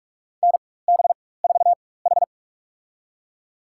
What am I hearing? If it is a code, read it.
Morse code, heard as a keyed tone, about 41 wpm, spaced as for 20 wpm, 700 Hz. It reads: NB4H
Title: USA Call Signs - 4 Characters - 4x Character Spacing - 45wpm